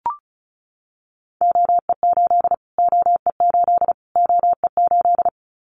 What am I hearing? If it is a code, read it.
Morse code, heard as a keyed tone, about 34 wpm, 700 Hz. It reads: OE8 OE8 OE8
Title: Callsign Prefixes for Top 25 DXCC - Repeated 3x - 35wpm